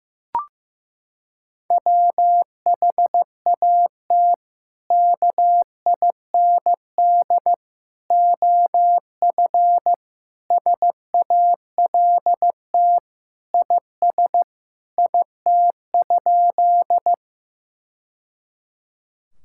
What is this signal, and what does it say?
Morse code, heard as a keyed tone, about 15 wpm, 700 Hz. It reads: WHAT KIND OF SALT IS IT?
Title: Sentences from Top 1000 Words - Review Part 5 - 15wpm